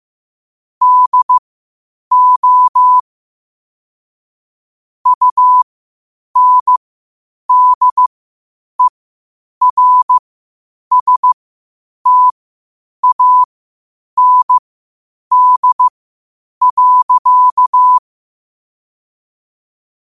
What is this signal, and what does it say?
Morse code, heard as a keyed tone, about 15 wpm, spaced as for 8 wpm, 1000 Hz. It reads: DO UNDERSTAND.
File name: qotdfiles / 8wpm